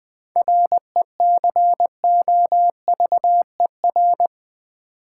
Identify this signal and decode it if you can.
Morse code, heard as a keyed tone, about 20 wpm, 700 Hz. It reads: RECOVER
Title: 7 Letter Words - First 3 Letters Given - 20wpm